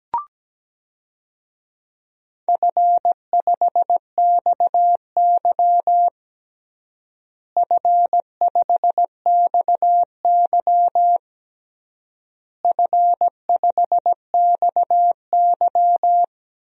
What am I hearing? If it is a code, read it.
Morse code, heard as a keyed tone, about 17 wpm, 700 Hz. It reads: F5XY F5XY F5XY
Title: DXCC-100 Call Signs - 4 Characters - Repeated 3x - 17wpm